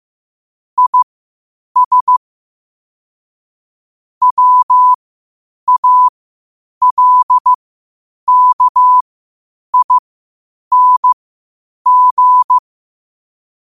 Morse code, heard as a keyed tone, about 15 wpm, spaced as for 8 wpm, 1000 Hz. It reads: IS WALKING
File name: qotdfiles / 8wpm